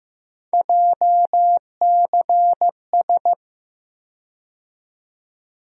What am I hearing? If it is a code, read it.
Morse code, heard as a keyed tone, about 15 wpm, 700 Hz. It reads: JCS
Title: USA Call Sign Postfixes - 3 Letters 15wpm